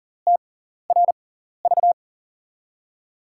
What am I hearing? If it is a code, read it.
Morse code, heard as a keyed tone, about 40 wpm, spaced as for 14 wpm, 700 Hz. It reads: TRV
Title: USA Call Signs - 5 Characters - 6x Character Spacing - 40wpm